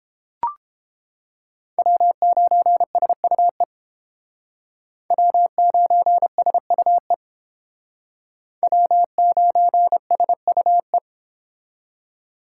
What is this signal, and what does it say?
Morse code, heard as a keyed tone, about 30 wpm, 700 Hz. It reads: W9SUE W9SUE W9SUE